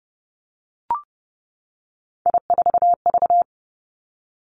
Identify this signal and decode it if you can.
Morse code, heard as a keyed tone, about 30 wpm, 700 Hz. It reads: I4V
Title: Three Challenging Letter-Number Combinations 30wpm